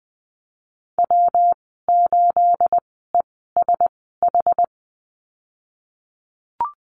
Morse code, heard as a keyed tone, about 19 wpm, spaced as for 14 wpm, 700 Hz. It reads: W8ESH
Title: 219 Lesson 35g - Number 0 - Call Signs 20wpm